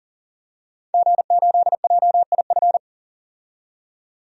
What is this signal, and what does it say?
Morse code, heard as a keyed tone, about 40 wpm, 700 Hz. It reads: G8JIF
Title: DXCC-100 Call Signs - 5 Characters - 40wpm